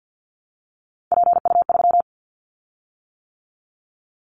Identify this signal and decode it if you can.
Morse code, heard as a keyed tone, about 50 wpm, 700 Hz. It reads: PU3
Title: Callsign Prefixes for Top 100 DXCC - 50wpm